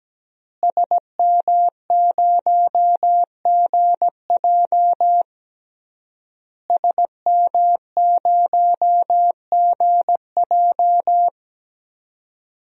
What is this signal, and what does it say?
Morse code, heard as a keyed tone, about 17 wpm, 700 Hz. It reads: SM0GJ SM0GJ
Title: DXCC-100 Call Signs - 5 Characters - Repeated 3x - 17wpm